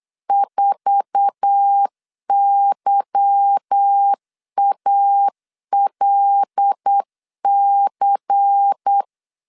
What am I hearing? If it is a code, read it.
Morse code, heard as a keyed tone, about 8 wpm, 800 Hz. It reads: 4YALC
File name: LEZIONI / 8 YJ45 / 8 YJ45 8wpm 3214